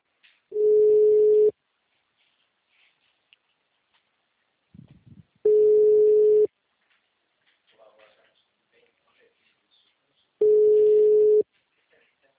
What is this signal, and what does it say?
Ringback tone (Europe, 425 Hz), with speech.
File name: out- -2003- - - .41533